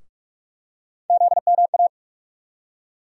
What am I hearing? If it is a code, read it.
Morse code, heard as a keyed tone, about 47 wpm, 700 Hz. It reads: ZMA
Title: USA Call Sign Postfixes - 3 Letters 45wpm